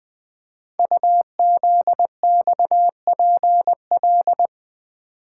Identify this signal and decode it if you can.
Morse code, heard as a keyed tone, about 20 wpm, 700 Hz. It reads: UZXPL
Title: Code Groups - 5 Letters - 20wpm